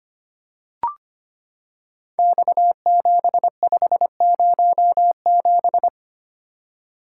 Morse code, heard as a keyed tone, about 25 wpm, 700 Hz. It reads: X7507
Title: Code Groups - 5 Letter:Numbers - 25wpm